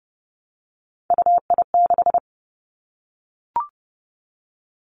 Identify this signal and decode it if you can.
Morse code, heard as a keyed tone, about 30 wpm, 700 Hz. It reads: UI6
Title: Three Challenging Letter-Number Combinations 30wpm